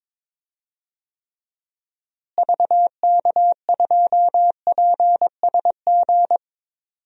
Morse code, heard as a keyed tone, about 22 wpm, 700 Hz. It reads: VK2PSG
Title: DXCC-100 Call Signs - 6 Characters - Repeated 2x - 22wpm